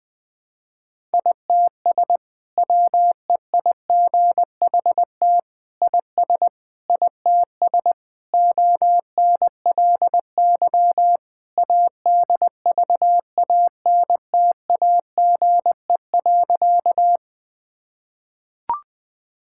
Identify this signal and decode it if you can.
Morse code, heard as a keyed tone, about 20 wpm, 700 Hz. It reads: ITS WEIGHT IS ITS ONLY ADVANTAGE.
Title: Sentences from Top 1000 Words - Review Part 6 - 20wpm